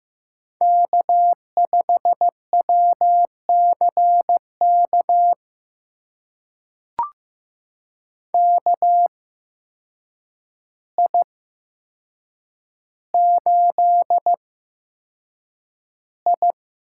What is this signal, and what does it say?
Morse code, heard as a keyed tone, about 15 wpm, spaced as for 7 wpm, 700 Hz. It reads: K5WCK KI8I
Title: USA Call Signs - 5 Characters - 8x Character Spacing - 15wpm